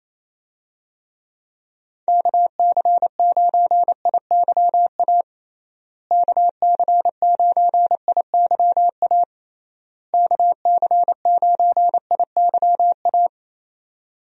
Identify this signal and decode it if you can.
Morse code, heard as a keyed tone, about 28 wpm, 700 Hz. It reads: KC9IYA KC9IYA KC9IYA
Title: USA Call Signs - 6 Characters - Repeated 3x - 28wpm